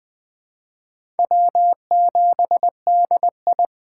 Morse code, heard as a keyed tone, about 20 wpm, 700 Hz. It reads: W7DI
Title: USA Call Signs - 4 Characters - 6x Character Spacing - 20wpm